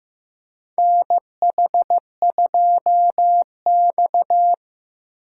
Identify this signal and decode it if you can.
Morse code, heard as a keyed tone, about 15 wpm, 700 Hz. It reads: NH2X
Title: USA Call Signs - 4 Characters - 3x Character Spacing - 15wpm